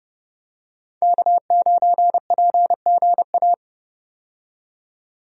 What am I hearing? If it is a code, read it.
Morse code, heard as a keyed tone, about 31 wpm, 700 Hz. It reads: K9PGA
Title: USA Call Signs - 5 Characters - 30wpm